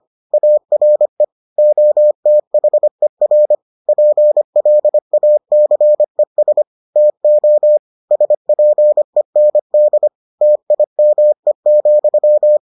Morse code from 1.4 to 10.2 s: OTHER PLACES TO SPEND

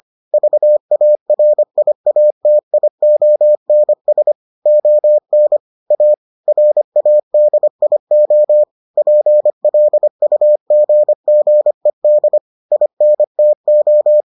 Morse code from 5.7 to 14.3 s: A RADIO PLUGGED INTO

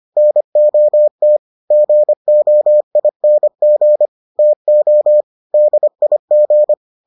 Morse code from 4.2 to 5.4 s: TO

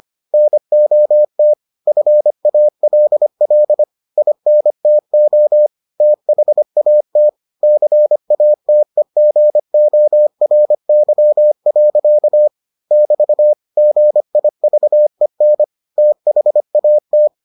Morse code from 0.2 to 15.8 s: NOT FALL INTO THAT CATEGORY. <BT> GIVEN

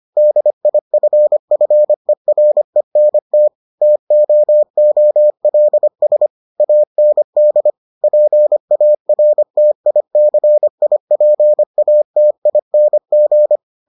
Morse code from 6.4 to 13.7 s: AND PARTICIPATING

